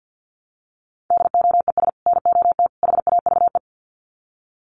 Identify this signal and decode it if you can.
Morse code, heard as a keyed tone, about 49 wpm, 700 Hz. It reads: DOES NOT HAVE
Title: Sets of 3 Words 50wpm